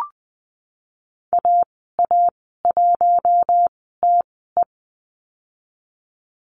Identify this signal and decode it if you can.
Morse code, heard as a keyed tone, about 20 wpm, 700 Hz. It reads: A A 1 T E
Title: 055 Lesson 10g - Letter r - Call Signs 20wpm